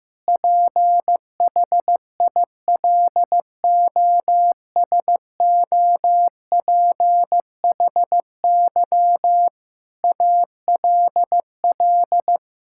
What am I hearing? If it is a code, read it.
Morse code, heard as a keyed tone, about 15 wpm, 700 Hz. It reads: PHILOSOPHY ALL
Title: Sentences from Top 1000 Words - Review Part 6 - 15wpm